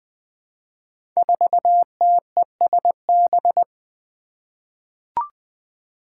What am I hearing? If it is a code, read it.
Morse code, heard as a keyed tone, about 20 wpm, 700 Hz. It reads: 4TESB
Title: Five Challenging Letter-Number Combinations 20wpm